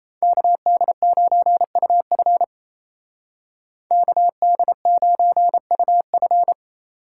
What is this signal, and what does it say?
Morse code, heard as a keyed tone, about 33 wpm, 700 Hz. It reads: KD9UF KD9UF
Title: Speed Racing - USA 2x2 Call Signs 28wpm